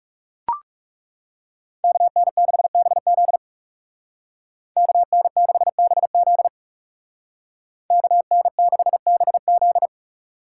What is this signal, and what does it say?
Morse code, heard as a keyed tone, about 39 wpm, 700 Hz. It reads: KN6BZ KN6BZ KN6BZ